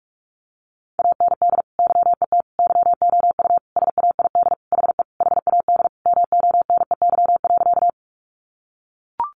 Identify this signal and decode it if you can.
Morse code, heard as a keyed tone, about 44 wpm, 700 Hz. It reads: AND YET YOU SAID HE HAD MONEY.
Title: Sentences from Top 500 Words 45wpm